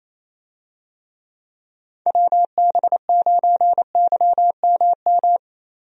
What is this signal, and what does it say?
Morse code, heard as a keyed tone, about 28 wpm, 700 Hz. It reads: WB9YMM
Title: USA Call Signs - 6 Characters - 28wpm